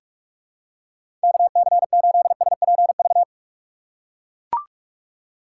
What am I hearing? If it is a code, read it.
Morse code, heard as a keyed tone, about 43 wpm, 700 Hz. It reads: KC8SPV
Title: USA Call Signs - 6 Characters - 4x Character Spacing - 45wpm